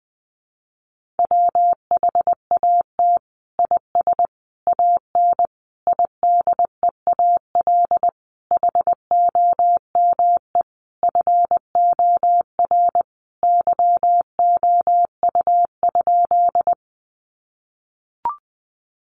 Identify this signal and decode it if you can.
Morse code, heard as a keyed tone, about 20 wpm, 700 Hz. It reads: WHAT IS AN IDEAL HOME FOR YOU?
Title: Sentences from Top 1400 Words 20wpm